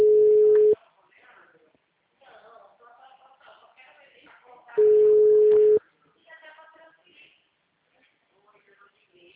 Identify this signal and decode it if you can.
Ringback tone (Europe, 425 Hz), with speech.